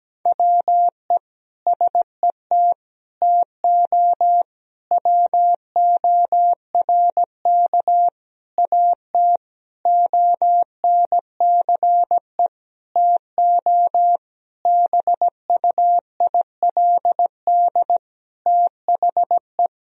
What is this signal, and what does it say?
Morse code, heard as a keyed tone, about 17 wpm, 700 Hz. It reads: WE SET TO WORK AT ONCE TO BUILD THE
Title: Sentences from Top 1000 Words - Review Part 1 - 17wpm